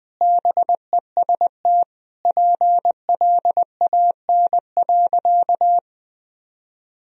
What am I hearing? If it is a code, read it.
Morse code, heard as a keyed tone, about 20 wpm, 700 Hz. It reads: BEST PLAN.
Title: Sentences from Top 1000 Words - Review Part 2 - 20wpm